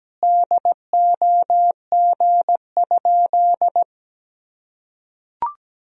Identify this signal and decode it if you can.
Morse code, heard as a keyed tone, about 17 wpm, 700 Hz. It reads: DOG?
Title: Sentences from Top 1000 Words - Review Part 2 - 17wpm